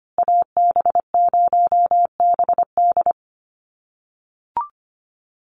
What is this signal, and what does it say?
Morse code, heard as a keyed tone, about 25 wpm, 700 Hz. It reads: AB0BD
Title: USA Call Signs Repeated 2x 25wpm